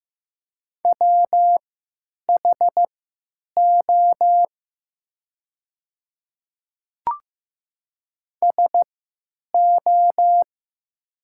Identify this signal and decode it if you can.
Morse code, heard as a keyed tone, about 15 wpm, spaced as for 9 wpm, 700 Hz. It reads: WHO SO
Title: 3x Character Spacing - Top 100 Words - 15wpm